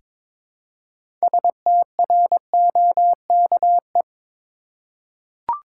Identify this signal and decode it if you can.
Morse code, heard as a keyed tone, about 22 wpm, 700 Hz. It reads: STROKE
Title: Six Letter Words - 22wpm